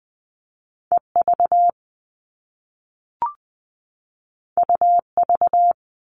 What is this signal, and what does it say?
Morse code, heard as a keyed tone, about 20 wpm, 700 Hz. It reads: E V U V
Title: All Cut Numbers - 2-Digit 20wpm